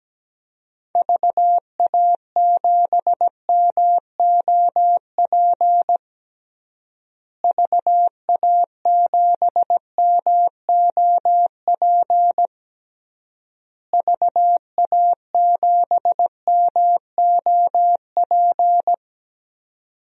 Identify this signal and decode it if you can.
Morse code, heard as a keyed tone, about 17 wpm, 700 Hz. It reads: VA7MOP VA7MOP VA7MOP
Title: DXCC-100 Call Signs - 6 Characters - Repeated 3x - 17wpm